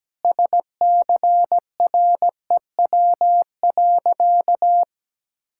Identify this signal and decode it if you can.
Morse code, heard as a keyed tone, about 17 wpm, 700 Hz. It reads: SCREW.
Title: Sentences from Top 1600 Words 17wpm